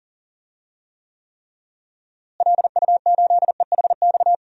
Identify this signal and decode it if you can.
Morse code, heard as a keyed tone, about 40 wpm, 700 Hz. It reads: LU8EHX
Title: DXCC-100 Call Signs - 6 Characters - Repeated 3x - 40wpm